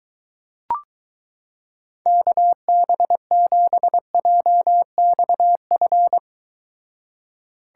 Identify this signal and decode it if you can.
Morse code, heard as a keyed tone, about 23 wpm, 700 Hz. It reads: KB7JXF